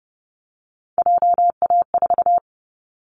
Morse code, heard as a keyed tone, about 29 wpm, 700 Hz. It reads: JA4
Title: Callsign Prefixes for Top 25 DXCC - Repeated 3x - 30wpm